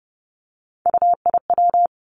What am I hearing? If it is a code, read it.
Morse code, heard as a keyed tone, about 30 wpm, 700 Hz. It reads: UIW
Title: Three Challenging Letter-Number Combinations 30wpm